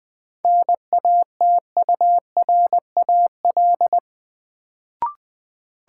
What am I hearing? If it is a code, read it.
Morse code, heard as a keyed tone, about 20 wpm, 700 Hz. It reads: NATURAL
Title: Sentences from Top 700 Words - New Word Drill - 20wpm